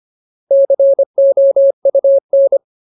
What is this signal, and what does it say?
Morse code, heard as a keyed tone, about 25 wpm, 550 Hz. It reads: COUN
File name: Ultimate Word List / 25WPM Ultimate